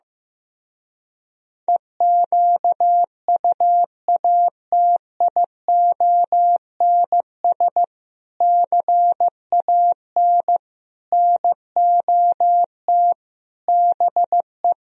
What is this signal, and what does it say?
Morse code, heard as a keyed tone, about 15 wpm, 700 Hz. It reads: EQUATIONS CAN NOT BE
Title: Speed-Racing - Sets of 5 Words 15wpm